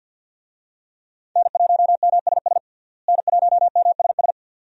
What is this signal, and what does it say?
Morse code, heard as a keyed tone, about 52 wpm, 700 Hz. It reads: N1MSS N1MSS
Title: USA Call Signs - 5 Characters - Repeated 2x - 50wpm